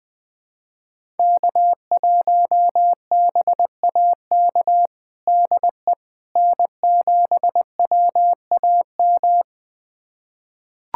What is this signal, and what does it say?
Morse code, heard as a keyed tone, about 20 wpm, 700 Hz. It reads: K1BAK DE N7WAM
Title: USA Callsign DE Callsign - 5 Characters - 20wpm